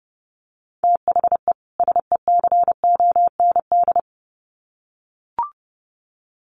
Morse code, heard as a keyed tone, about 30 wpm, 700 Hz. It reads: THE SECOND
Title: Sets of 2 Words 30wpm